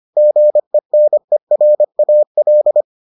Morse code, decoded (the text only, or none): GENERAL